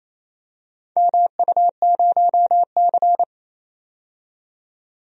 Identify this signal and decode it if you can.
Morse code, heard as a keyed tone, about 28 wpm, 700 Hz. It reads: MU0C
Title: DXCC-100 Call Signs - 4 Characters - 28wpm